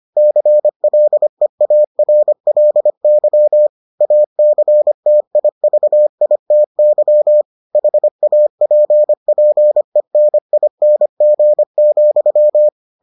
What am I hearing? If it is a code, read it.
Morse code, heard as a keyed tone, about 25 wpm, 600 Hz. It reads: CLEARLY ACTIVITY HAPPENING,